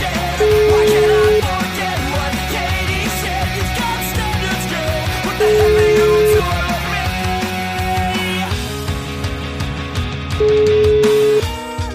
Ringback tone (Europe, 425 Hz), with noise.